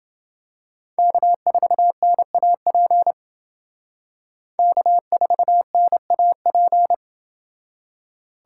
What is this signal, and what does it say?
Morse code, heard as a keyed tone, about 29 wpm, 700 Hz. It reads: K4NAP K4NAP